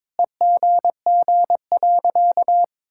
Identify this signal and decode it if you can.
Morse code, heard as a keyed tone, about 21 wpm, 700 Hz. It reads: EGG.
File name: Sentences with 7-Word Phrases - 22wpm